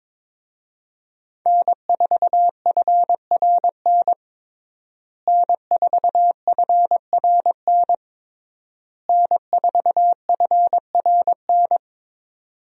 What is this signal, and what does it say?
Morse code, heard as a keyed tone, about 22 wpm, 700 Hz. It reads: N4FRN N4FRN N4FRN
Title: USA Call Signs - 5 Characters - Repeated 3x - 22wpm